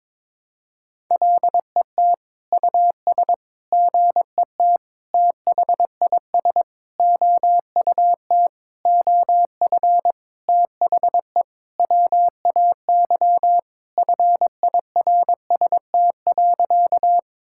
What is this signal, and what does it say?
Morse code, heard as a keyed tone, about 22 wpm, 700 Hz. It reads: LET US GET THIS OUT OF THE WAY FIRST.
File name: Sentences with 7-Word Phrases - 22wpm